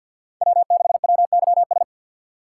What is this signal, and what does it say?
Morse code, heard as a keyed tone, about 52 wpm, 700 Hz. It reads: W6WYS
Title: USA Call Signs - 5 Characters - 4x Character Spacing - 50wpm